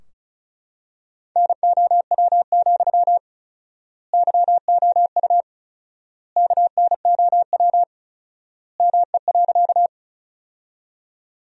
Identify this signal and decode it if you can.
Morse code, heard as a keyed tone, about 34 wpm, 700 Hz. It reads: NOW, YOU KNOW ME.
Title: 4x Word-Spacing - Sentences from Top 200 Words - 35wpm